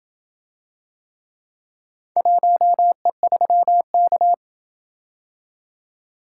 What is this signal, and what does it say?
Morse code, heard as a keyed tone, about 26 wpm, 700 Hz. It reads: 1E3K